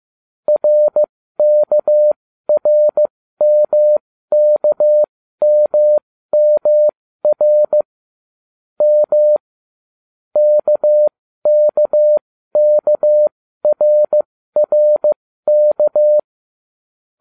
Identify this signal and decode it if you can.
Morse code, heard as a keyed tone, about 15 wpm, spaced as for 13 wpm, 600 Hz. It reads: RKRMKMMR M KKKRRK